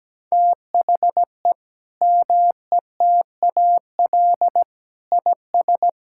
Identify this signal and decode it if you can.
Morse code, heard as a keyed tone, about 17 wpm, 700 Hz. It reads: THE METAL IS
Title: Sentences from Top 1000 Words - Review Part 6 - 17wpm